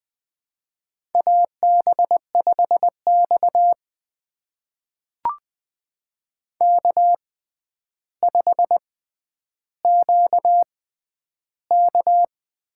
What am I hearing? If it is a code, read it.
Morse code, heard as a keyed tone, about 20 wpm, spaced as for 10 wpm, 700 Hz. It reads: AB5X K5QK
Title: USA Call Signs - 4 Characters - 6x Character Spacing - 20wpm